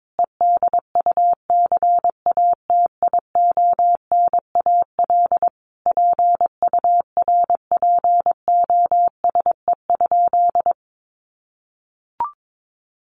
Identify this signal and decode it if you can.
Morse code, heard as a keyed tone, about 22 wpm, 700 Hz. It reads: EDUCATIONAL PURPOSE?
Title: Sentences from Top 1200 Words 22wpm